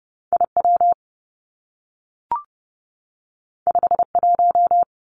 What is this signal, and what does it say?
Morse code, heard as a keyed tone, about 30 wpm, 700 Hz. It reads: I W 5 1
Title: Two Challenging Letter-Number Combinations 30wpm